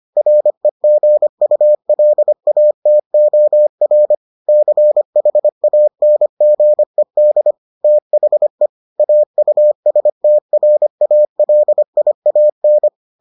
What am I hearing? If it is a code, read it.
Morse code, heard as a keyed tone, about 25 wpm, 600 Hz. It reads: REGULATOR CHANGED THE AUSTRALIAN